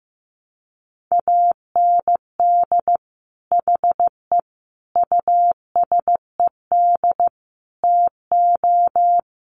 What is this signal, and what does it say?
Morse code, heard as a keyed tone, about 15 wpm, 700 Hz. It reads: AND HE USED TO
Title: Sentences from Top 1200 Words 15wpm